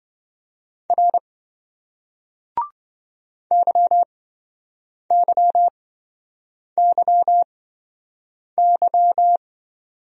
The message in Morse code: R YYYY

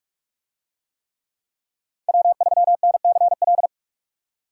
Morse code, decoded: W3NCL

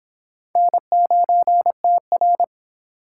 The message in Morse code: N9TR

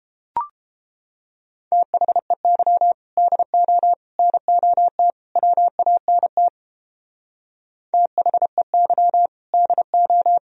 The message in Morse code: THEY DO NOT WANT THEY DO

